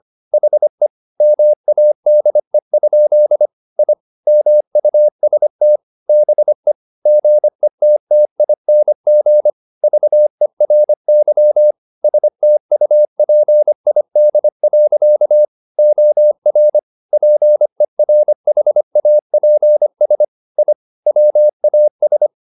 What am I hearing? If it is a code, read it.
Morse code, heard as a keyed tone, about 25 wpm, 600 Hz. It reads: HE MADE? I MUST BE GETTING VERY STUPID. OR PERHAPS I WAS
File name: the old man and the sea part 1 25wpm